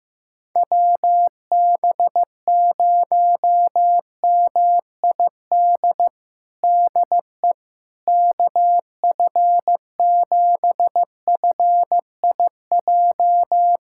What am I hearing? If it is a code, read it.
Morse code, heard as a keyed tone, about 16 wpm, 700 Hz. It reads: WB0MID DE KF7FIJ
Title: USA Callsign DE Callsign - 6 Characters - 15wpm